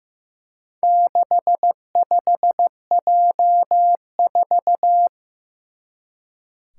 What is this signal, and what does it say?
Morse code, heard as a keyed tone, about 15 wpm, 700 Hz. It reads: 65J4
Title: Four Challenging Letter-Number Combinations 15wpm